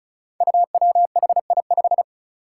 Morse code, heard as a keyed tone, about 36 wpm, 700 Hz. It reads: UWHI5
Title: Five Challenging Letter-Number Combinations 35wpm